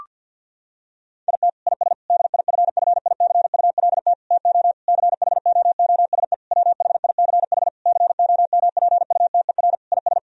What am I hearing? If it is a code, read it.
Morse code, heard as a keyed tone, about 50 wpm, 700 Hz. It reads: IT IS DIFFICULT TO CHOOSE WHICH COMPUTER IS
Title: Sentences from Top 1000 Words - Review Part 3 - 50wpm